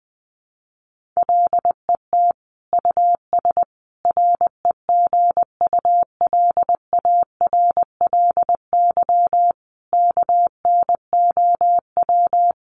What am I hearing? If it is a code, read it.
Morse code, heard as a keyed tone, about 20 wpm, 700 Hz. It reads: LET US REGULARLY KNOW
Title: Sentences from Top 1400 Words 20wpm